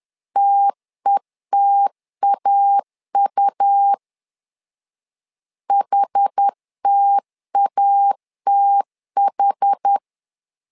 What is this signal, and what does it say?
Morse code, heard as a keyed tone, about 10 wpm, 800 Hz. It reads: TETAU HTATH
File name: LEZIONI / 2 UHDN / 2 UH 10wpm 1714